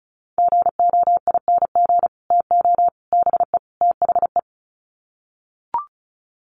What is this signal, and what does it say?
Morse code, heard as a keyed tone, about 34 wpm, 700 Hz. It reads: GOING TO BE THE
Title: Sets of 4 Words 35wpm